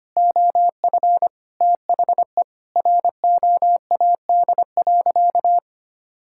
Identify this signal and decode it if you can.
Morse code, heard as a keyed tone, about 25 wpm, 700 Hz. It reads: OF THE ROAD.
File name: practice / 25wpm / Sentences with 4-Word Phrases - 25wpm